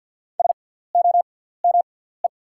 Morse code, heard as a keyed tone, about 48 wpm, spaced as for 17 wpm, 700 Hz. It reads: SOME
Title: 6x Character Spacing - Top 300 Words - 50wpm